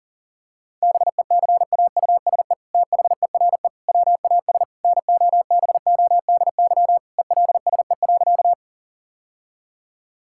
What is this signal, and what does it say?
Morse code, heard as a keyed tone, about 40 wpm, 700 Hz. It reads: BECAUSE THERE WAS NOBODY ELSE.
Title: Sentences from Top 1600 Words 40wpm